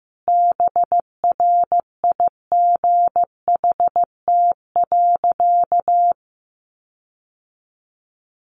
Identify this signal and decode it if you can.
Morse code, heard as a keyed tone, about 14 wpm, 700 Hz. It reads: BRIGHT.
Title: Sentences from Top 1500 Words 15wpm